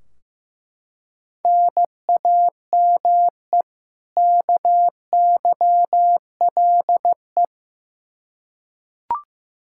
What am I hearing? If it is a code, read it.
Morse code, heard as a keyed tone, about 15 wpm, 700 Hz. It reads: NAME KYLE
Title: Practice QSO - Exchange 1 - Beginning - 15wpm